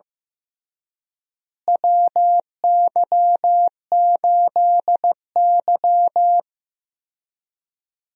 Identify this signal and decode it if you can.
Morse code, heard as a keyed tone, about 15 wpm, 700 Hz. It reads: WY8Y
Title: USA Call Signs - 4 Characters - Repeated 3x - 15wpm